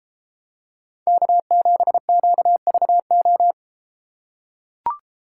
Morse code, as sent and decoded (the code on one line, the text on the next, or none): -.- --... --.- ...- ---
K7QVO